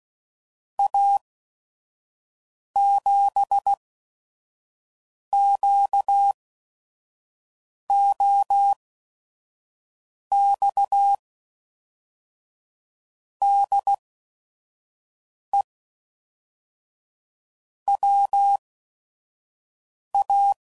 Morse code, heard as a keyed tone, about 16 wpm, spaced as for 5 wpm, 800 Hz. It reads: A7QOX DE WA
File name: CW Code Groups @5WPM / QSOs / QSO WA7QOX 1